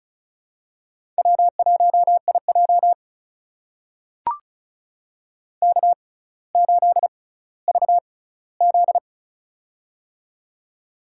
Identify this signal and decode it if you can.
Morse code, heard as a keyed tone, about 34 wpm, spaced as for 18 wpm, 700 Hz. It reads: W1IJ K8VZ
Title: USA Call Signs - 4 Characters - 6x Character Spacing - 35wpm